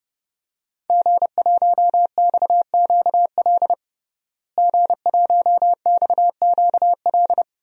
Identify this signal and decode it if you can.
Morse code, heard as a keyed tone, about 32 wpm, 700 Hz. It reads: G1XQL G1XQL
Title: DXCC-100 Call Signs - 5 Characters - Repeated 3x - 30wpm